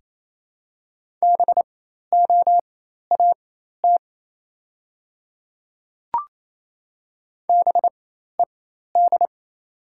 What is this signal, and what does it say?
Morse code, heard as a keyed tone, about 28 wpm, spaced as for 13 wpm, 700 Hz. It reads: BOAT BED
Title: 4x Character Spacing - Top 500 Words - 28wpm